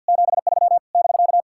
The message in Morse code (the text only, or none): 73 <BK>